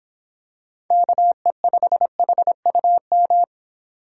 Morse code, heard as a keyed tone, about 26 wpm, 700 Hz. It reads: KE5HUM